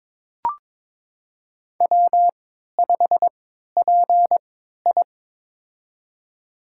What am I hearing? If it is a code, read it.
Morse code, heard as a keyed tone, about 22 wpm, spaced as for 13 wpm, 700 Hz. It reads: W5PI
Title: USA Call Signs - 4 Characters - 3x Character Spacing - 22wpm